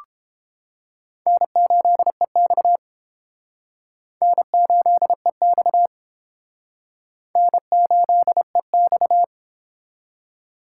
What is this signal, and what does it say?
Morse code, heard as a keyed tone, about 32 wpm, 700 Hz. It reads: N8EX N8EX N8EX